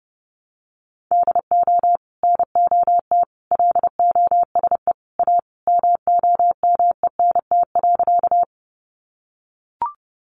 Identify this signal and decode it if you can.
Morse code, heard as a keyed tone, about 30 wpm, 700 Hz. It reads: DO NOT LOSE A MOMENT.
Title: Sentences from Top 1100 Words 30wpm